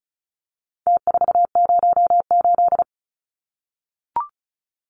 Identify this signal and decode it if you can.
Morse code, heard as a keyed tone, about 34 wpm, 700 Hz. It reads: T408
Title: Numbers - Contest Style - 4-Digit 35wpm